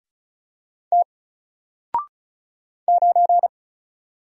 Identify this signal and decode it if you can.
Morse code, heard as a keyed tone, about 36 wpm, 700 Hz. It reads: T 9
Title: Single Character (Letter:Number:Punctuation:Prosign) Alt 35wpm